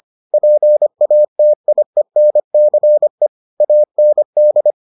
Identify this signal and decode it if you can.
Morse code, heard as a keyed tone, about 25 wpm, 600 Hz. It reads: PATIENCE AND